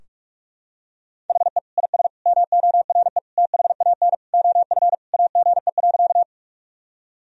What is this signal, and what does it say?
Morse code, heard as a keyed tone, about 46 wpm, 700 Hz. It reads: HE IS MORE THAN OF AGE.
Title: Sentences from Top 1000 Words - Review Part 3 - 45wpm